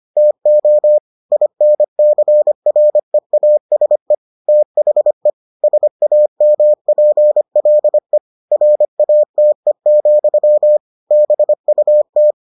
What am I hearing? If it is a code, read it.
Morse code, heard as a keyed tone, about 25 wpm, 600 Hz. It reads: TO INCREASE THE SAMPLE RATE, BUT